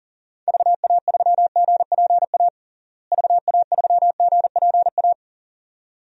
Morse code, heard as a keyed tone, about 40 wpm, 700 Hz. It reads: VA3GPA VA3GPA
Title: DXCC-100 Call Signs - 6 Characters - Repeated 2x - 40wpm